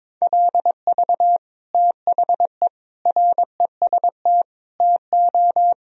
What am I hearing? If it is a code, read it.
Morse code, heard as a keyed tone, about 22 wpm, 700 Hz. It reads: LV THE REST TO
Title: Sentences from Top 500 Words and Abbreviations 22wpm